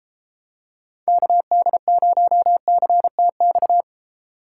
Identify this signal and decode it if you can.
Morse code, heard as a keyed tone, about 30 wpm, 700 Hz. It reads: KD0CTX